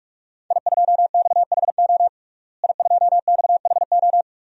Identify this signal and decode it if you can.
Morse code, heard as a keyed tone, about 46 wpm, 700 Hz. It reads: I2XHO I2XHO
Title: DXCC-100 Call Signs - 5 Characters - Repeated 3x - 45wpm